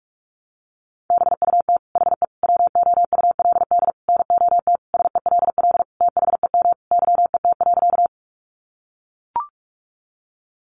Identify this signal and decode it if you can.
Morse code, heard as a keyed tone, about 44 wpm, 700 Hz. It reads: BUT HE WOULD NOT SELL THEM YET.
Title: Sentences from Top 1100 Words 45wpm